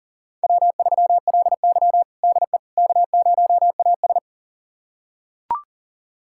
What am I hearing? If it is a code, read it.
Morse code, heard as a keyed tone, about 40 wpm, 700 Hz. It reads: W3LY DE K0AS
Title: USA Callsign DE Callsign - 4 Characters - 40wpm